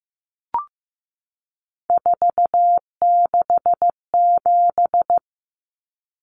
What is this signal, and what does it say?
Morse code, heard as a keyed tone, about 15 wpm, 700 Hz. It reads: 467
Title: Numbers - 3-Digit 15wpm